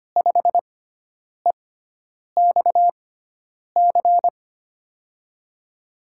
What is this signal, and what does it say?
Morse code, heard as a keyed tone, about 25 wpm, spaced as for 9 wpm, 700 Hz. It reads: 5EXC
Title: USA Call Signs - 6 Characters - 6x Character Spacing - 25wpm